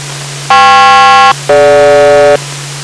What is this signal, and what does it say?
Aircraft SELCAL, heard as two pairs of tones. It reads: KP-DG